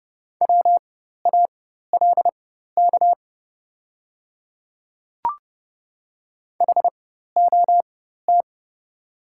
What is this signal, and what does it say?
Morse code, heard as a keyed tone, about 30 wpm, spaced as for 14 wpm, 690 Hz. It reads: WALK HOT
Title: 4x Character Spacing - Top 300 Words - 30wpm